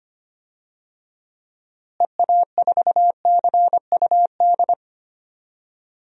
Morse code, heard as a keyed tone, about 25 wpm, 700 Hz. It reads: EA4CUD
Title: DXCC-100 Call Signs - 6 Characters - Repeated 2x - 25wpm